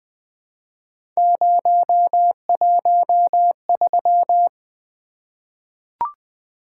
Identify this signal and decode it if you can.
Morse code, heard as a keyed tone, about 20 wpm, 700 Hz. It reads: 013
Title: Three Letter-Number Combinations - 20wpm